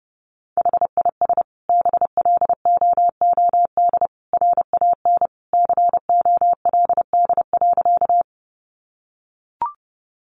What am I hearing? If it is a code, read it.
Morse code, heard as a keyed tone, about 30 wpm, 700 Hz. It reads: HIS BLOOD RAN COLD.
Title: Sentences from Top 1100 Words 30wpm